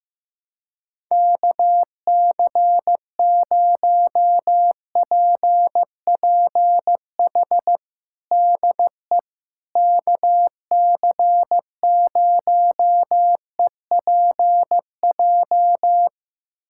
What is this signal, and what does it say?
Morse code, heard as a keyed tone, about 15 wpm, 700 Hz. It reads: KC0PPH DE KC0EPJ
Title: USA Callsign DE Callsign 15wpm